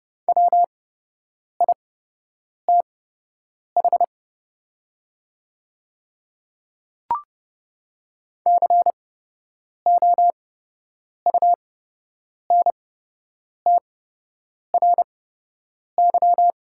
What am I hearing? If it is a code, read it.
Morse code, heard as a keyed tone, about 30 wpm, spaced as for 8 wpm, 690 Hz. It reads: WITH COUNTRY
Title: 8x Character Spacing - Top 500 Words - 30wpm